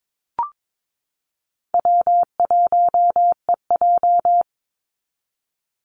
Morse code, heard as a keyed tone, about 22 wpm, 700 Hz. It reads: W1EJ
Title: NA SOTA Activator Callsigns 22wpm